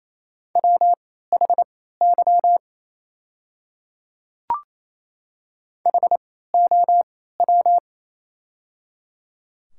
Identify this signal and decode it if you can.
Morse code, heard as a keyed tone, about 28 wpm, spaced as for 16 wpm, 700 Hz. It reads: WHY HOW
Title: 3x Character Spacing - Top 300 Words - 28wpm